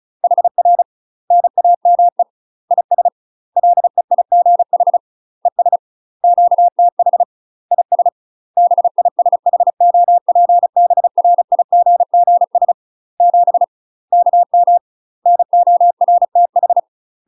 Morse code, heard as a keyed tone, about 34 wpm, 700 Hz. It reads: HR NAME IS LEIGH ES QTH IS BISHOPBRIGGS 7 KM NORTH
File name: UK MASTER QSOs TRAINING / 35WPM / UK MASTER QSOs 0008